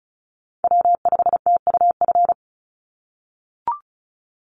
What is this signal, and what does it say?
Morse code, heard as a keyed tone, about 36 wpm, 700 Hz. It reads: W5TUF
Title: Call Signs - 35wpm